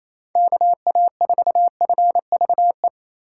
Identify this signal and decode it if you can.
Morse code, heard as a keyed tone, about 28 wpm, 700 Hz. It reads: KA4FVE
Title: USA Call Signs - 6 Characters - 2x Character Spacing - 28wpm